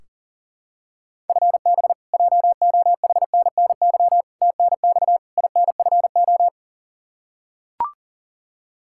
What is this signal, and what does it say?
Morse code, heard as a keyed tone, about 40 wpm, 700 Hz. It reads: FB JOHNNY TNX INFO
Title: Practice QSO - Exchange 3 - More Info - 40wpm